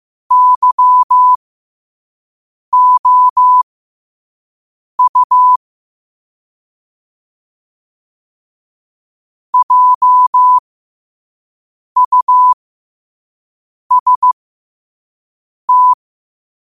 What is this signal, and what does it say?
Morse code, heard as a keyed tone, about 15 wpm, spaced as for 5 wpm, 1000 Hz. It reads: YOU JUST
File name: qotdfiles / 5wpm